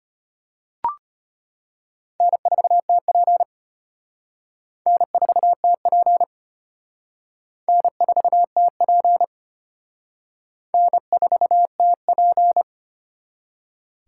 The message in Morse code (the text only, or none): N4TP N4TP N4TP N4TP